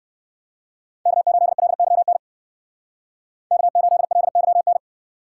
Morse code, heard as a keyed tone, about 53 wpm, 700 Hz. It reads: K8LYN K8LYN
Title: Speed Racing - USA 1x3 Call Signs 45wpm